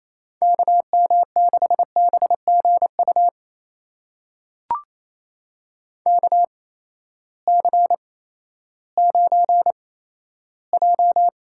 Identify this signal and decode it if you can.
Morse code, heard as a keyed tone, about 28 wpm, spaced as for 14 wpm, 700 Hz. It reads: KM6BGU KC9J
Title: USA Call Signs - 6 Characters - 8x Character Spacing - 28wpm